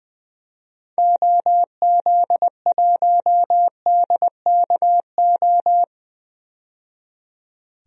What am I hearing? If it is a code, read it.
Morse code, heard as a keyed tone, about 21 wpm, 700 Hz. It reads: OZ1DKO
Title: DXCC-100 Call Signs - 6 Characters - 20wpm